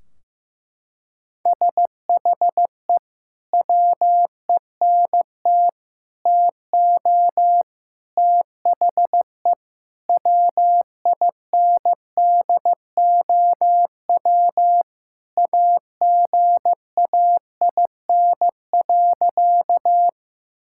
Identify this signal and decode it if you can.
Morse code, heard as a keyed tone, about 15 wpm, 700 Hz. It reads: SHE WENT TO THE WINDOW AGAIN.
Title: Sentences from Top 1600 Words 15wpm